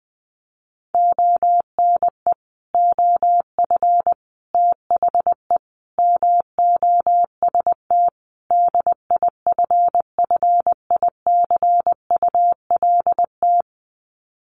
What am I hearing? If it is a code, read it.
Morse code, heard as a keyed tone, about 20 wpm, 700 Hz. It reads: ONE OF THE MOST DIFFICULT
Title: Sets of 5 Words Encore2 20wpm